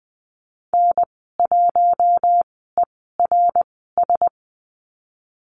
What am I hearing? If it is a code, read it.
Morse code, heard as a keyed tone, about 20 wpm, spaced as for 14 wpm, 700 Hz. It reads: N1ERS
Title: 062 Lesson 11g - Letter h - Call Signs 20wpm